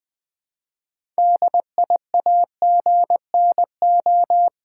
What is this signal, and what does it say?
Morse code, heard as a keyed tone, about 20 wpm, 700 Hz. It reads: DIAGNO
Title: Build A Word - 3 Letters At A Time - 8-Letter Words - 20wpm